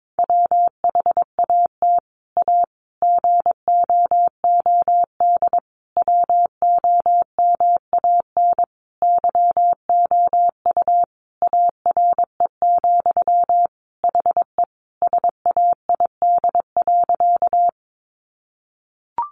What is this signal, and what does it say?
Morse code, heard as a keyed tone, about 22 wpm, 700 Hz. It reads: WHAT A GOOD WOMAN YOU ARE, HE SAID.
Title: Sentences from Top 1100 Words 22wpm